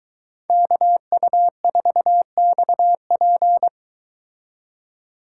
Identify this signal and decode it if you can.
Morse code, heard as a keyed tone, about 22 wpm, 700 Hz. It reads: KU4XP